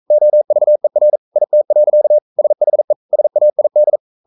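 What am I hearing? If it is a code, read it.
Morse code, heard as a keyed tone, about 42 wpm, 600 Hz. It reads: OVER IT. SHE SAID
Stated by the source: Morse code sending of an English novel at 42 wpm.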